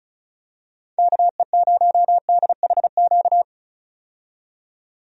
Morse code, heard as a keyed tone, about 34 wpm, 700 Hz. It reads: KE0DHQ
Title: USA Call Signs - 6 Characters - 35wpm